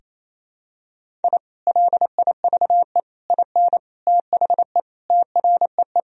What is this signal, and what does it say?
Morse code, heard as a keyed tone, about 28 wpm, 700 Hz. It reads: I LIVE IN THE TREE
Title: Sentences from Top 1000 Words - Review Part 1 - 28wpm